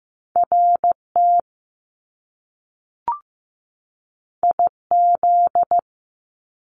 Morse code, heard as a keyed tone, about 15 wpm, 700 Hz. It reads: RT IZ
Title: Top Two Letter Combinations in Words 15wpm